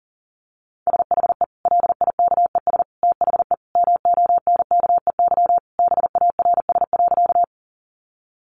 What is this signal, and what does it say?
Morse code, heard as a keyed tone, about 40 wpm, 700 Hz. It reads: SHE LIKES THE MONKEY BARS.
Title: Easy Phrases 40wpm